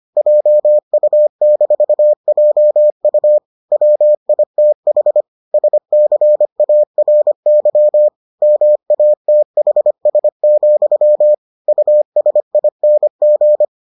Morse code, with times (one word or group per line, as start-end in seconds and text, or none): 0.2-3.4 JU-JU
3.7-5.2 WITH
5.5-8.1 SCARY
8.4-11.3 MATHS,
11.7-13.6 USING